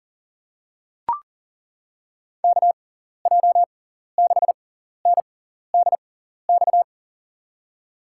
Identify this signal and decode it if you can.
Morse code, heard as a keyed tone, about 40 wpm, spaced as for 14 wpm, 700 Hz. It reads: KJ6NDX
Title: USA Call Signs - 6 Characters - 6x Character Spacing - 40wpm